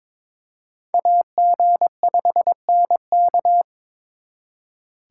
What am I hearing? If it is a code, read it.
Morse code, heard as a keyed tone, about 26 wpm, 700 Hz. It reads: AG5NK
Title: Speed Racing - USA 2x2 Call Signs 22wpm